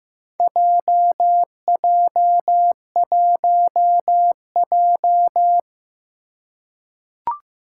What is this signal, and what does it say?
Morse code, heard as a keyed tone, about 15 wpm, 700 Hz. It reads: JJ1J
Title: Four Challenging Letter-Number Combinations 15wpm